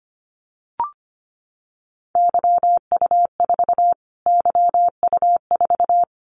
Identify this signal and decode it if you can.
Morse code, heard as a keyed tone, about 25 wpm, 700 Hz. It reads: YU4 YU4
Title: Callsign Prefixes for Top 25 DXCC - Repeated 3x - 25wpm